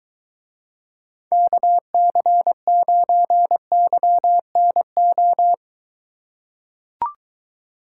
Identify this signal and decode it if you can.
Morse code, heard as a keyed tone, about 23 wpm, 700 Hz. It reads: KC9YNO